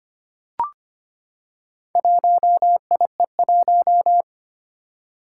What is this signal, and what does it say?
Morse code, heard as a keyed tone, about 25 wpm, 700 Hz. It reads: 1IE1
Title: Four Challenging Letter-Number Combinations 25wpm